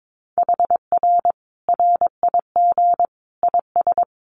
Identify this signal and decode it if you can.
Morse code, heard as a keyed tone, about 22 wpm, 700 Hz. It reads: HR RIG IS
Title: Common QSO Phrases 22wpm